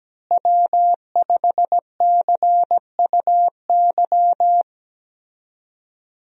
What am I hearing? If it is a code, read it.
Morse code, heard as a keyed tone, about 17 wpm, 700 Hz. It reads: W5CUY
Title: USA Call Signs - 5 Characters - Repeated 2x - 17wpm